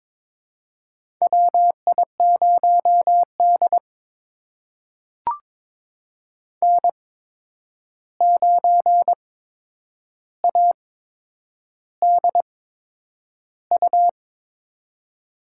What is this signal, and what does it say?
Morse code, heard as a keyed tone, about 22 wpm, spaced as for 9 wpm, 700 Hz. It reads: WI0D N9ADU
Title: USA Call Signs - 8x Character Spacing - 22wpm